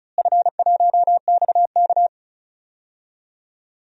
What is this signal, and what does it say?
Morse code, heard as a keyed tone, about 34 wpm, 700 Hz. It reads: F1XK
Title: DXCC-100 Call Signs - 4 Characters - Repeated 2x - 35wpm